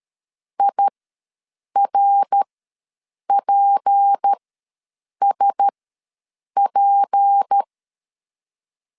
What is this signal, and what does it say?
Morse code, heard as a keyed tone, about 13 wpm, spaced as for 7 wpm, 800 Hz. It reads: IRPSP
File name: LEZIONI / 6 PKX0 / 6 P ws 12wpm 3117